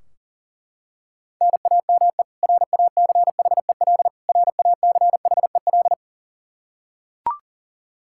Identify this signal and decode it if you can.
Morse code, heard as a keyed tone, about 40 wpm, 700 Hz. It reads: NAME RACHEL RACHEL
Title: Practice QSO - Exchange 1 - Beginning - 40wpm